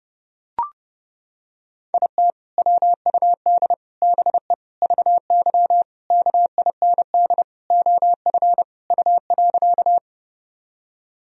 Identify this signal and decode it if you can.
Morse code, heard as a keyed tone, about 30 wpm, 700 Hz. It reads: IT WUD BE VY KIND OF U.
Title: Sentences from Top 200 Words and Abbreviations 30wpm